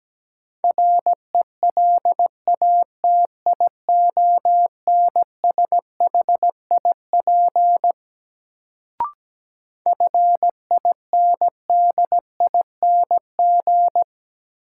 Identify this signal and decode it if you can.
Morse code, heard as a keyed tone, about 17 wpm, 700 Hz. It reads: RELATIONSHIP FINDING
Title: Sentences from Top 1100 Words - New Word Drill - 17wpm